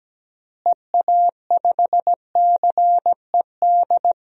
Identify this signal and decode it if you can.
Morse code, heard as a keyed tone, about 17 wpm, 700 Hz. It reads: EA5CED
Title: DXCC-100 Call Signs - 6 Characters - Repeated 3x - 17wpm